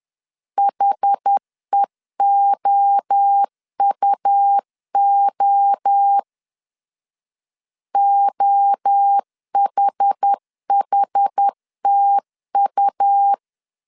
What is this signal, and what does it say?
Morse code, heard as a keyed tone, about 10 wpm, 800 Hz. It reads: HEOUO OHHTU
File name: CW / DET / 1 3 1500C S B 10wpm 3335